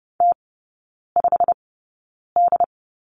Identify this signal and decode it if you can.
Morse code, heard as a keyed tone, about 30 wpm, spaced as for 9 wpm, 700 Hz. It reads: T5D
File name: Lesson 16e - Letter c - Three Characters 30wpm